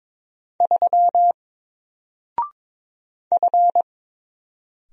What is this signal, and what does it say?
Morse code, heard as a keyed tone, about 22 wpm, 700 Hz. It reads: 3 F
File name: Single-Letter-Number-22wpm